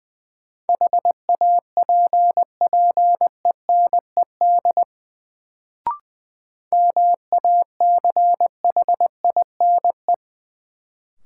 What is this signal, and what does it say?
Morse code, heard as a keyed tone, about 20 wpm, 700 Hz. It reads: HAPPENED MACHINE
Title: Sentences from Top 1000 Words - Review Part 2 - New Word Drill - 20wpm